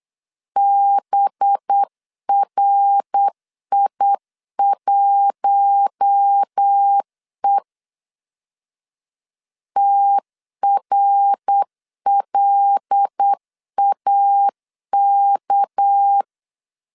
Morse code, heard as a keyed tone, about 8 wpm, 800 Hz. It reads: BRI1E TRLAK
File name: LEZIONI / 6 PKX0 / 6 PK 8wpm 3003